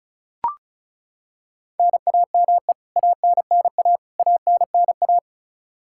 Morse code, heard as a keyed tone, about 34 wpm, 700 Hz. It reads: NAME ANNA ANNA
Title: Practice QSO - Exchange 1 - Beginning - 35wpm